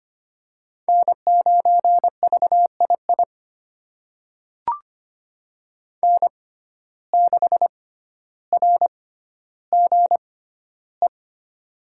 Morse code, heard as a keyed tone, about 25 wpm, spaced as for 13 wpm, 700 Hz. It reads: N9VII N6RGE
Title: USA Call Signs - 5 Characters - 6x Character Spacing - 25wpm